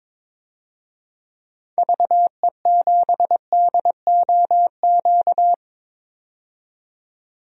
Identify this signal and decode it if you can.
Morse code, heard as a keyed tone, about 22 wpm, 700 Hz. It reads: VE7DOQ
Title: DXCC-100 Call Signs - 6 Characters - 22wpm